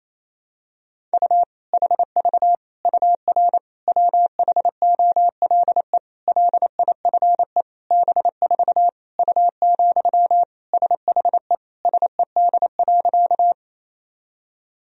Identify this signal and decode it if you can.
Morse code, heard as a keyed tone, about 27 wpm, 700 Hz. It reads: U HV UR WHOLE LIFE B4 U, SHE SED.
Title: Sentences from Top 400 Words and Abbreviations 28wpm